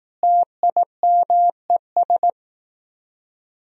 Morse code, heard as a keyed tone, about 18 wpm, 700 Hz. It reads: TIMES
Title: Speed-Racing - Sets of 5 Words 15wpm